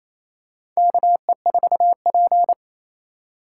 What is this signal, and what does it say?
Morse code, heard as a keyed tone, about 28 wpm, 700 Hz. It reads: KE4P
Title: USA Call Signs - 4 Characters - 2x Character Spacing - 28wpm